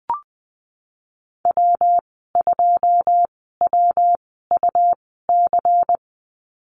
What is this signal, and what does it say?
Morse code, heard as a keyed tone, about 20 wpm, spaced as for 15 wpm, 700 Hz. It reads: W2WUC
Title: 128 Lesson 21g - Number 6 - Call Signs 20wpm